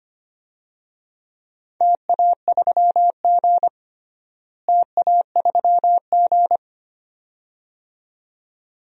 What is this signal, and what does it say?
Morse code, heard as a keyed tone, about 25 wpm, 700 Hz. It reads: TA3G TA3G
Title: DXCC-100 Call Signs - 4 Characters - Repeated 2x - 25wpm